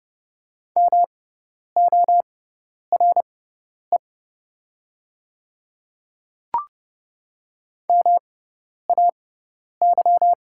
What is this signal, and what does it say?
Morse code, heard as a keyed tone, about 30 wpm, spaced as for 10 wpm, 700 Hz. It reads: MORE MAY
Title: 6x Character Spacing - Top 100 Words - 30wpm